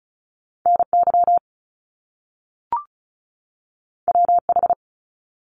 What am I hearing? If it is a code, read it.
Morse code, heard as a keyed tone, about 36 wpm, 700 Hz. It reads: NY WH
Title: Top Two Letter Combinations in Words 35wpm